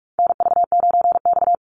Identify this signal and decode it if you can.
Morse code, heard as a keyed tone, about 44 wpm, 700 Hz. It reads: NV9X
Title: Call Signs Repeated 3x - 45wpm